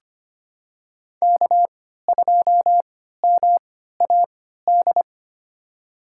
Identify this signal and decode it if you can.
Morse code, heard as a keyed tone, about 25 wpm, spaced as for 14 wpm, 700 Hz. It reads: K2MAD
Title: USA Call Signs - 5 Characters - 3x Character Spacing - 25wpm